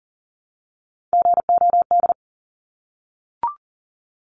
Morse code, heard as a keyed tone, about 40 wpm, 700 Hz. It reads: GOD
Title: Top 1751-2000 Words 40wpm